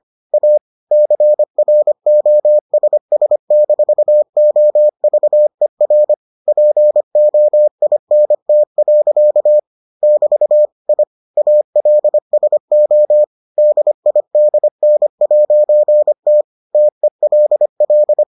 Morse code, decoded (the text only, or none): A CROSS-OVER POINT. <BT> I ALSO DIDN'T TELL